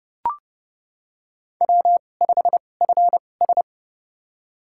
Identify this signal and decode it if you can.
Morse code, heard as a keyed tone, about 30 wpm, spaced as for 22 wpm, 700 Hz. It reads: W5FS
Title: USA Call Signs - 4 Characters - 2x Character Spacing - 30wpm